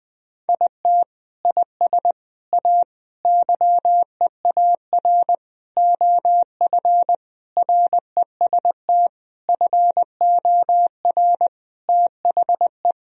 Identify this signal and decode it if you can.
Morse code, heard as a keyed tone, about 20 wpm, 700 Hz. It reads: IT IS A YEAR OF REST FOR THE
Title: Sentences from Top 1000 Words - Review Part 1 - 20wpm